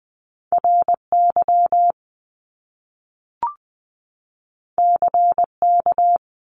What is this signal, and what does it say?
Morse code, heard as a keyed tone, about 20 wpm, 700 Hz. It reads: RY CK
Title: Top Two Letter Combinations in Words 20wpm